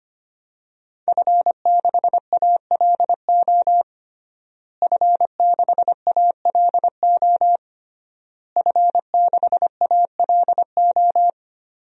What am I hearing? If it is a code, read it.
Morse code, heard as a keyed tone, about 25 wpm, 700 Hz. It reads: F6ALO F6ALO F6ALO
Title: DXCC-100 Call Signs - 5 Characters - Repeated 3x - 25wpm